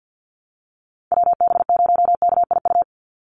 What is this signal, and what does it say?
Morse code, heard as a keyed tone, about 49 wpm, 700 Hz. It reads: WB0KIU
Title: NA SOTA Chaser Callsigns 50wpm